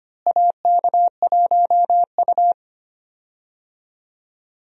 Morse code, heard as a keyed tone, about 25 wpm, 700 Hz. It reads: AK1U
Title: USA Call Signs - 4 Characters - Repeated 2x - 25wpm